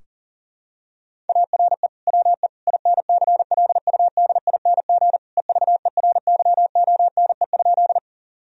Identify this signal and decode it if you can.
Morse code, heard as a keyed tone, about 40 wpm, 700 Hz. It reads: ARE WE INCLUDING EVERYONE?
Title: Sentences from Top 1000 Words - Review Part 4 - 40wpm